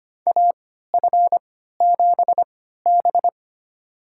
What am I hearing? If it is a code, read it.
Morse code, heard as a keyed tone, about 25 wpm, spaced as for 14 wpm, 700 Hz. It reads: AF7B
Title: USA Call Signs - 4 Characters - 3x Character Spacing - 25wpm